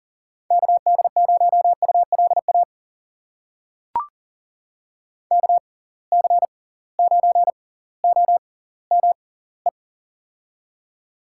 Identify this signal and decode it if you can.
Morse code, heard as a keyed tone, about 40 wpm, spaced as for 20 wpm, 700 Hz. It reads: KD0ULA KC9OME
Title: USA Call Signs - 6 Characters - 6x Character Spacing - 40wpm